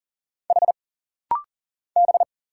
Morse code, heard as a keyed tone, about 40 wpm, 700 Hz. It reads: H B
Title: Single Letters - Rapid-Fire 40wpm